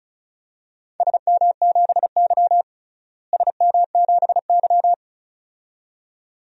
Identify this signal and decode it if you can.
Morse code, heard as a keyed tone, about 36 wpm, 700 Hz. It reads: SM7Y SM7Y
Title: DXCC-100 Call Signs - 4 Characters - Repeated 2x - 35wpm